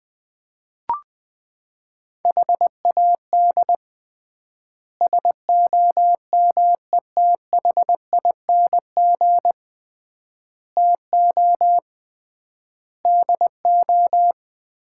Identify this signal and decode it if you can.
Morse code, heard as a keyed tone, about 21 wpm, 700 Hz. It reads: HAD SOMETHING TO DO
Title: 3x Word-Spacing - Sets of 5 Words - 20wpm